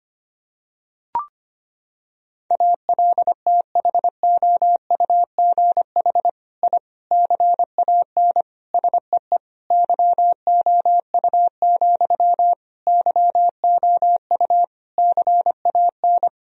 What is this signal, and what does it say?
Morse code, heard as a keyed tone, about 25 wpm, 700 Hz. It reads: ALTHOUGH I CAN SEE YOU, YOU CAN
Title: Sentences from Top 1000 Words - Review Part 3 - 25wpm